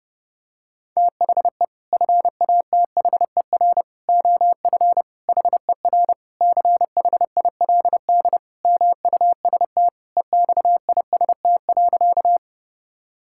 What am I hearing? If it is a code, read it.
Morse code, heard as a keyed tone, about 30 wpm, 700 Hz. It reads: THE FATHER OF HER CHILD MUST EXIST.
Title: Sentences from Top 1000 Words - Review Part 5 - 30wpm